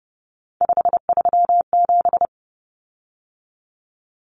Morse code, heard as a keyed tone, about 30 wpm, 700 Hz. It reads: 537
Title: Common Cut Numbers - 3-Digit 30wpm